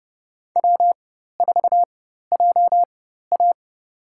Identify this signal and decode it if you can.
Morse code, heard as a keyed tone, about 30 wpm, spaced as for 14 wpm, 700 Hz. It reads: W4JA
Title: USA Call Signs - 4x Character Spacing - 30wpm